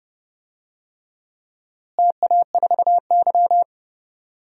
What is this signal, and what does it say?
Morse code, heard as a keyed tone, about 31 wpm, 700 Hz. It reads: TA4Y
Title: DXCC-100 Call Signs - 4 Characters - Repeated 3x - 30wpm